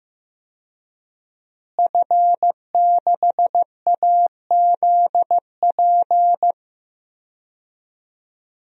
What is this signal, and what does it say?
Morse code, heard as a keyed tone, about 15 wpm, 700 Hz. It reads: F6AZP
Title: DXCC-100 Call Signs - 5 Characters - Repeated 2x - 15wpm